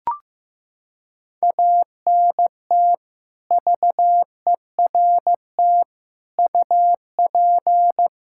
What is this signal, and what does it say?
Morse code, heard as a keyed tone, about 15 wpm, 700 Hz. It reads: ANT VERT UP
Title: Practice QSO - Exchange 2 - Info - 15wpm